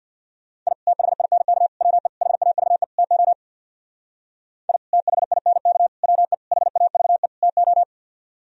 Morse code, heard as a keyed tone, about 63 wpm, 700 Hz. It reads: I THINK WE HAVE TO I THINK WE HAVE TO